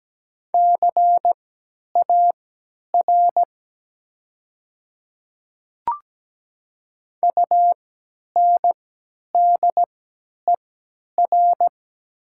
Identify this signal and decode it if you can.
Morse code, heard as a keyed tone, about 17 wpm, spaced as for 10 wpm, 700 Hz. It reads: CAR UNDER
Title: 3x Character Spacing - Top 300 Words - 17wpm